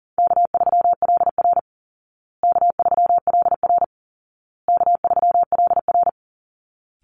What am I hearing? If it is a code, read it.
Morse code, heard as a keyed tone, about 37 wpm, 700 Hz. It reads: K3LR K3LR K3LR
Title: Easy Call Signs Repeated 3x - 40wpm